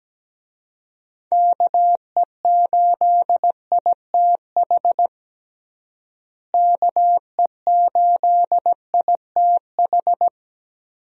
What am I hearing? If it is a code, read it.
Morse code, heard as a keyed tone, about 17 wpm, 700 Hz. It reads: KE8ITH KE8ITH
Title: USA Call Signs - 6 Characters - Repeated 2x - 17wpm